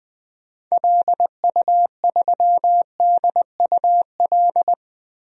Morse code, heard as a keyed tone, about 20 wpm, 700 Hz. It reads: LU3DUL
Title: DXCC-100 Call Signs - 6 Characters - 20wpm